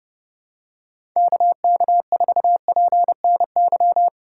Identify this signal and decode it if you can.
Morse code, heard as a keyed tone, about 31 wpm, 700 Hz. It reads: KK4PNY
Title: USA Call Signs - 6 Characters - 3x Character Spacing - 30wpm